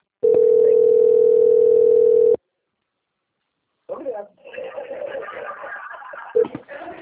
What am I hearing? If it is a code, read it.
Ringback tone (North America), 440 + 480 Hz, with speech.
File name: out- - - - - .37615